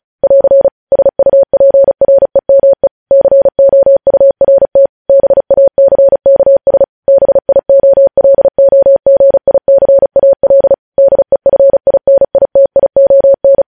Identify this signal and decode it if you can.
Morse code, heard as a keyed tone, about 36 wpm, 550 Hz. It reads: <AR> SUPREME COURT BACKS BIOLOGICAL DEFINITION
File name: podcast / News headlines-1x-35x35